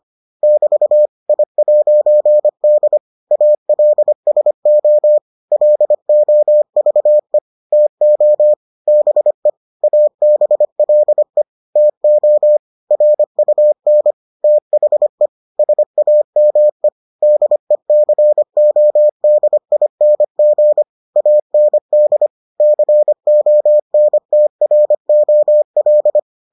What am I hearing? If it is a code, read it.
Morse code, heard as a keyed tone, about 25 wpm, 600 Hz. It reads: <BT> I'D ALSO LOVE TO BE ABLE TO RUN THE SAME DECODING AND CONTROL